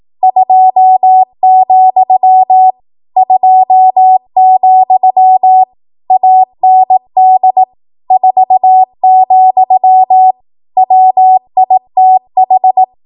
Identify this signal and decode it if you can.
Morse code, heard as a keyed tone, about 18 wpm, 750 Hz. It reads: 2, 2, AND 4, WITH
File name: Morse / 18WPM